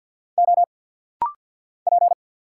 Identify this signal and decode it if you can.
Morse code, heard as a keyed tone, about 50 wpm, 690 Hz. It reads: O P
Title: Single Letter-Number Alt - Rapid-Fire 50wpm